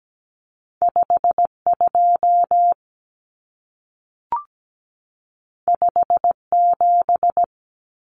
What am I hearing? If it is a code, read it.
Morse code, heard as a keyed tone, about 17 wpm, 700 Hz. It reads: 52 57
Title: Common Cut Numbers - 2-Digit 17wpm